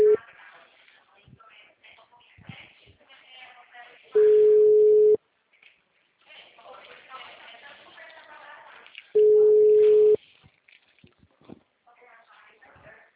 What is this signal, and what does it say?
Ringback tone (Europe, 425 Hz), with speech.
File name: out- -2012- - - .66018